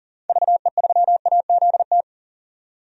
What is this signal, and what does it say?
Morse code, heard as a keyed tone, about 40 wpm, 700 Hz. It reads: VE3AZT
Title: DXCC-100 Call Signs - 6 Characters - 40wpm